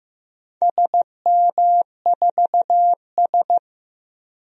Morse code, heard as a keyed tone, about 15 wpm, 700 Hz. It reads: SM4S
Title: DXCC-100 Call Signs - 4 Characters - Repeated 3x - 15wpm